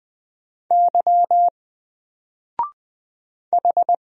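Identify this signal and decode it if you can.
Morse code, heard as a keyed tone, about 20 wpm, 700 Hz. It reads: Y H
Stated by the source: Single letter and number Morse drill at 20 wpm, from N4LCD.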